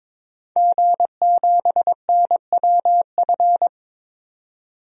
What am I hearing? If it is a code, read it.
Morse code, heard as a keyed tone, about 22 wpm, 700 Hz. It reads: G7NWF
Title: DXCC-100 Call Signs - 5 Characters - 22wpm